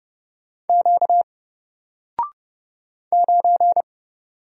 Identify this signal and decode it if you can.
Morse code, heard as a keyed tone, about 30 wpm, 700 Hz. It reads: Q 9
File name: Single-Letter-Number-30wpm